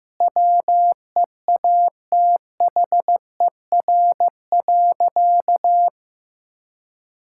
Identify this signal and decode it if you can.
Morse code, heard as a keyed tone, about 15 wpm, 700 Hz. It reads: WEATHER.
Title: Sentences from Top 1000 Words - Review Part 5 - 15wpm